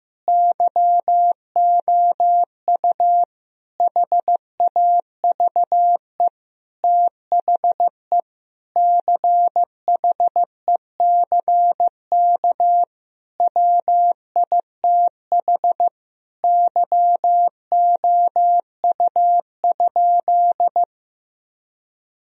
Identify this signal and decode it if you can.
Morse code, heard as a keyed tone, about 16 wpm, 700 Hz. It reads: YOU HAVE THE CHECK WITH YOU?
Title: Sentences from Top 1000 Words - Review Part 3 - 15wpm